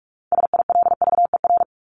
Morse code, heard as a keyed tone, about 44 wpm, 700 Hz. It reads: SILVER
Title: Top 1751-2000 Words 45wpm